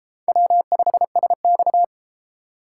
Morse code, heard as a keyed tone, about 31 wpm, 700 Hz. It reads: W5SX